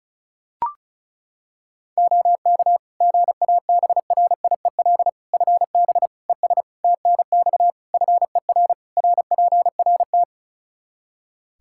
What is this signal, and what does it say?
Morse code, heard as a keyed tone, about 34 wpm, 700 Hz. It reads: OK GABRIEL FB ES TNX FER RPRT
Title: Practice QSO - Exchange 2 - Info - 35wpm